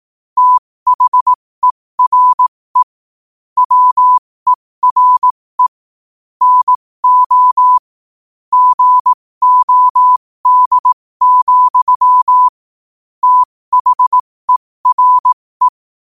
Morse code, heard as a keyed tone, about 18 wpm, spaced as for 15 wpm, 1000 Hz. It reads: THERE WERE NO GOD, THERE
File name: qotdfiles / 15wpm